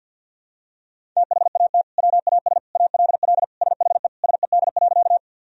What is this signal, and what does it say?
Morse code, heard as a keyed tone, about 51 wpm, 700 Hz. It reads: THAT WAS ALL SHE SED.
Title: Sentences from Top 100 Words and Abbreviations 50wpm